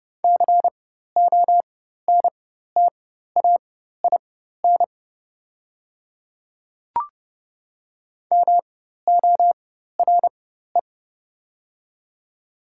Morse code, heard as a keyed tone, about 30 wpm, spaced as for 14 wpm, 700 Hz. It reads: CONTAIN MORE
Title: 4x Character Spacing - Top 500 Words - 30wpm